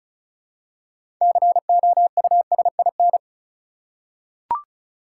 Morse code, heard as a keyed tone, about 34 wpm, 700 Hz. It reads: COUSIN
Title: Six Letter Words - 35wpm